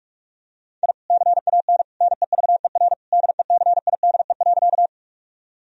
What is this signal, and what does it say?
Morse code, heard as a keyed tone, about 46 wpm, 700 Hz. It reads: I CAN NEVER DECIDE.
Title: Sentences from Top 1000 Words - Review Part 3 - 45wpm